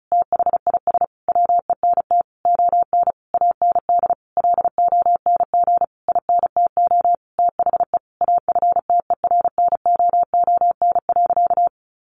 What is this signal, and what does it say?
Morse code, heard as a keyed tone, about 34 wpm, 700 Hz. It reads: THIS WENT ON AND LONG INTO THE AFTERNOON.
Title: Sentences from Top 1500 Words 35wpm